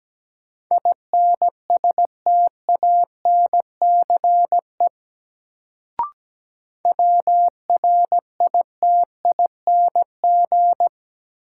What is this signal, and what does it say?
Morse code, heard as a keyed tone, about 17 wpm, 700 Hz. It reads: INSTANCE WRITING
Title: Sentences from Top 800 Words - New Word Drill - 17wpm